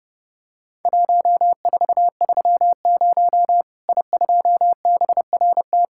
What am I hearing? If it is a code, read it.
Morse code, heard as a keyed tone, about 32 wpm, 700 Hz. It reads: 1430 I2BRT
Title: Worked All Europe DX Contest - QTC Exchange - 30wpm